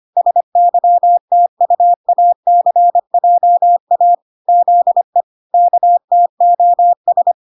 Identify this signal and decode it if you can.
Morse code, heard as a keyed tone, about 25 wpm, 700 Hz. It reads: SYTUACJA ZE KTOS